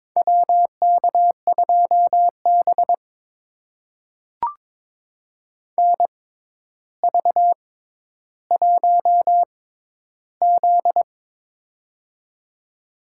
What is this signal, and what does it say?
Morse code, heard as a keyed tone, about 23 wpm, spaced as for 11 wpm, 700 Hz. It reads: WK2B NV1Z
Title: USA Call Signs - 4 Characters - 6x Character Spacing - 22wpm